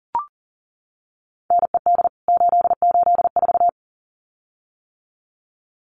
Morse code, heard as a keyed tone, about 40 wpm, 700 Hz. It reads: NED 884
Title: CWT Exchange 40wpm